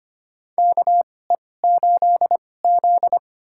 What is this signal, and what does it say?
Morse code, heard as a keyed tone, about 25 wpm, spaced as for 18 wpm, 700 Hz. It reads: KE8Z
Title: USA Call Signs - 2x Character Spacing - 25wpm